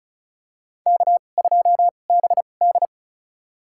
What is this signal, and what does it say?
Morse code, heard as a keyed tone, about 34 wpm, spaced as for 25 wpm, 700 Hz. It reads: K2BD
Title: USA Call Signs - 2x Character Spacing - 35wpm